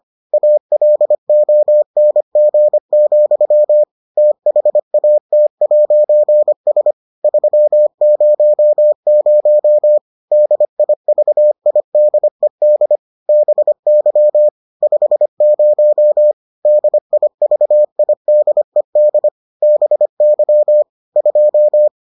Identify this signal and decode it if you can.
Morse code, heard as a keyed tone, about 25 wpm, 600 Hz. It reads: ALONG, THAT'S 300 DIVIDED BY 50 DIVIDED BY 2